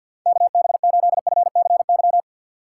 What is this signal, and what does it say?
Morse code, heard as a keyed tone, about 51 wpm, 700 Hz. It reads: KB8FCY
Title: USA Call Signs - 6 Characters - 4x Character Spacing - 50wpm